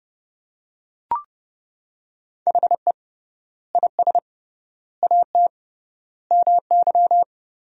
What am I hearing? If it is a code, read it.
Morse code, heard as a keyed tone, about 30 wpm, 700 Hz. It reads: HE IS AT MY
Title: 3x Word-Spacing - Sentences from Top 200 Words - 30wpm